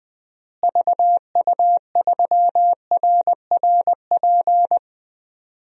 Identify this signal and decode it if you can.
Morse code, heard as a keyed tone, about 21 wpm, 700 Hz. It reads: VU3RRP
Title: DXCC-100 Call Signs - 6 Characters - Repeated 2x - 20wpm